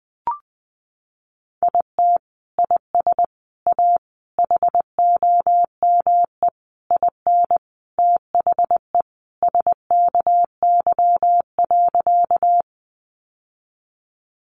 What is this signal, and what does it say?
Morse code, heard as a keyed tone, about 20 wpm, 700 Hz. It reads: IT IS A HOME IN THE SKY.
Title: Sentences from Top 1200 Words 20wpm